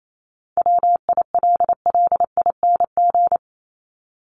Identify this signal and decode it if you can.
Morse code, heard as a keyed tone, about 28 wpm, 700 Hz. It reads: WILLING
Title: Top 751-1000 Words 28wpm